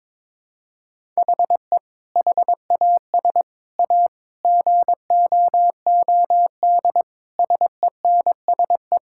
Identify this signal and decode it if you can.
Morse code, heard as a keyed tone, about 22 wpm, 700 Hz. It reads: HE HAS A GOOD SENSE
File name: practice / 22wpm / Sentences with 2-Word Phrases - 22wpm